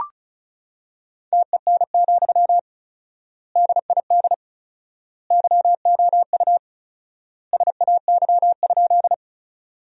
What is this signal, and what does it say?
Morse code, heard as a keyed tone, about 36 wpm, 700 Hz. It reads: TEN, DID YOU SAY?
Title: 4x Word-Spacing - Sentences from Top 400 Words - 35wpm